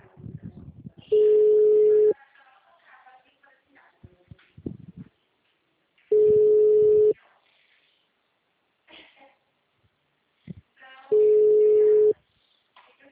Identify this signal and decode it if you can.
Ringback tone (Europe, 425 Hz), with speech.
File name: out- -2001- - - .580